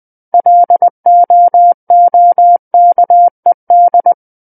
Morse code, heard as a keyed tone, about 20 wpm, 700 Hz. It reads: LOOKED
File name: 20 WPM